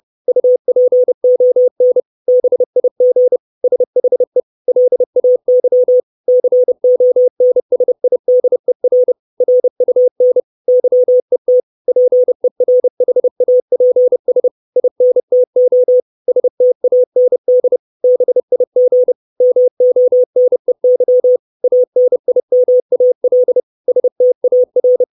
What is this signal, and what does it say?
Morse code, heard as a keyed tone, about 30 wpm, 500 Hz. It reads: UPON BIG SHE LAY CONSIDER RUN YET PERHAPS INTO STAND BIG MONEY ANIMAL STAR
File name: 632 Words 30WPM